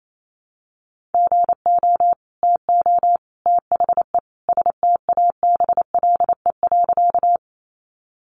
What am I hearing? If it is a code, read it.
Morse code, heard as a keyed tone, about 28 wpm, 700 Hz. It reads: GO TO THE STABLE.
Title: Sentences from Top 1500 Words 28wpm